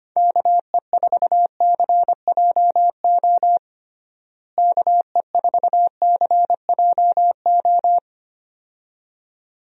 Morse code, heard as a keyed tone, about 25 wpm, 700 Hz. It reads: KE4CJO KE4CJO
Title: USA Call Signs - 6 Characters - Repeated 2x - 25wpm